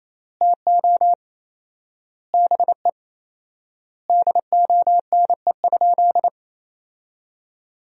Morse code, heard as a keyed tone, about 28 wpm, 700 Hz. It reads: TO BE DONE?
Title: 4x Word-Spacing - Sentences from Top 500 Words - 28wpm